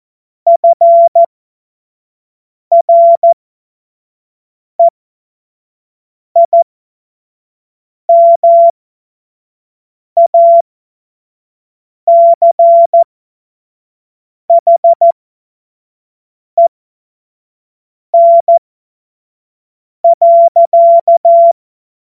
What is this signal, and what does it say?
Morse code, heard as a keyed tone, about 14 wpm, spaced as for 5 wpm, 680 Hz. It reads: FREIMACHEN.
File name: Morsekurs / Witze / Witz 09 April 2024 25 F70